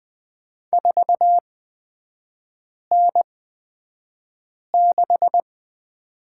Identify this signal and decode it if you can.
Morse code, heard as a keyed tone, about 20 wpm, spaced as for 5 wpm, 690 Hz. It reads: 4N6
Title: 229 Lesson 37e - Letter x - Three Characters 20wpm